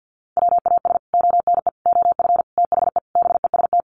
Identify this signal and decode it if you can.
Morse code, heard as a keyed tone, about 48 wpm, 700 Hz. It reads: WAS ONE OF THE BEST
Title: Sets of 5 Words Encore2 50wpm